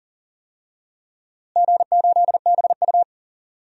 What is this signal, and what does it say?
Morse code, heard as a keyed tone, about 40 wpm, 700 Hz. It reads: G8BU
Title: DXCC-100 Call Signs - 4 Characters - Repeated 3x - 40wpm